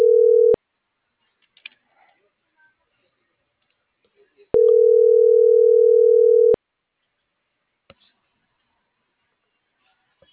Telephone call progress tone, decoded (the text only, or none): ringback tone